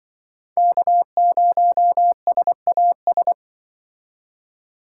Morse code, heard as a keyed tone, about 25 wpm, 700 Hz. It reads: K0SAS
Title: Speed Racing - USA Call Signs 20wpm